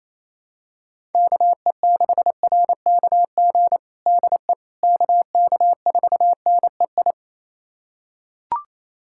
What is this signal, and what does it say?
Morse code, heard as a keyed tone, about 28 wpm, 700 Hz. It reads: KE6RKG DE KK4NEI
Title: USA Callsign DE Callsign 28wpm